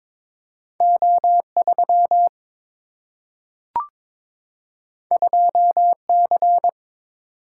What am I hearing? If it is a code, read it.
Morse code, heard as a keyed tone, about 22 wpm, 700 Hz. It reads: O3 2C
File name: practice / 22wpm / Two Letter-Number Combinations - 22wpm